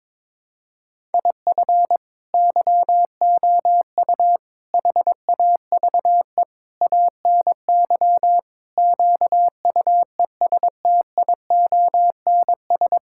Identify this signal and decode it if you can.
Morse code, heard as a keyed tone, about 22 wpm, 700 Hz. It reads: IF YOU HAVE ANY QUESTIONS
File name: practice / 22wpm / Sets of 8 Words - 22wpm